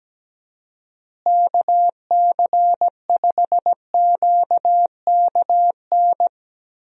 Morse code, heard as a keyed tone, about 17 wpm, 700 Hz. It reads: KC5QKN
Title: USA Call Signs - 6 Characters - Repeated 3x - 17wpm